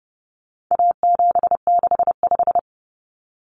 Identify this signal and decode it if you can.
Morse code, heard as a keyed tone, about 30 wpm, 700 Hz. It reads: A765
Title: Common Cut Numbers - 4-Digit 30wpm